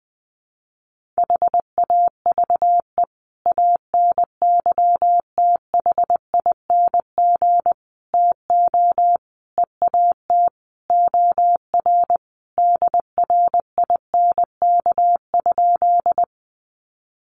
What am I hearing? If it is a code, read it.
Morse code, heard as a keyed tone, about 20 wpm, 700 Hz. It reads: HAVE ANYTHING TO EAT OR DRINK?
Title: Sentences from Top 1500 Words 20wpm